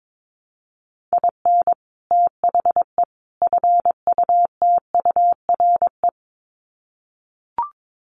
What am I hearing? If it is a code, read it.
Morse code, heard as a keyed tone, about 22 wpm, 700 Hz. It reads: IN THE FUTURE
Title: Sets of 3 Words 22wpm